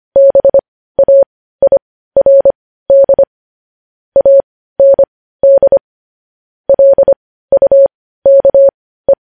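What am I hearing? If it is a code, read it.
Morse code, heard as a keyed tone, about 25 wpm, spaced as for 15 wpm, 550 Hz. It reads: BAIRD AND LUKE
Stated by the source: Morse news headlines practice, possibly sent at 15 wpm.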